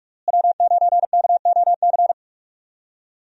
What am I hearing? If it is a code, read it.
Morse code, heard as a keyed tone, about 46 wpm, 700 Hz. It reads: W9KOC
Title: USA Call Signs - 5 Characters - Repeated 3x - 45wpm